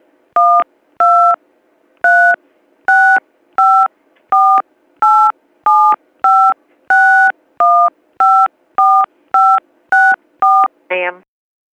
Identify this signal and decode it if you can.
DTMF, heard as telephone touch tones, tones of about 310 ms. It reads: 1236548756154564